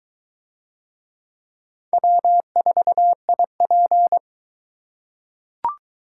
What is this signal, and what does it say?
Morse code, heard as a keyed tone, about 22 wpm, 700 Hz. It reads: W4IP